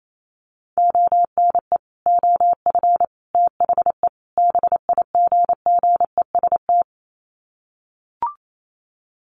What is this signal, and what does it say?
Morse code, heard as a keyed tone, about 28 wpm, 700 Hz. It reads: ONE OF THE BIGGEST
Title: Sets of 4 Words 28wpm